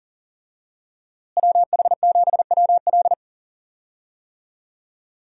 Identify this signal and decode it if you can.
Morse code, heard as a keyed tone, about 40 wpm, 700 Hz. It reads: WH7WL
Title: USA Call Signs - 5 Characters - Repeated 3x - 40wpm